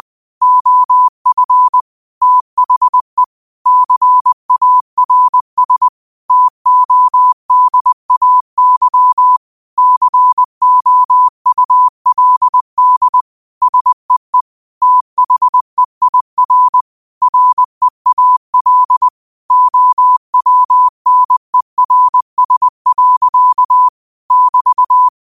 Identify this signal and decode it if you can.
Morse code, heard as a keyed tone, about 20 wpm, 1000 Hz. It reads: OF THE CARS TODAY COULD SEE THEIR REAL OWNERS. <BT>
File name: qotdfiles / 20wpm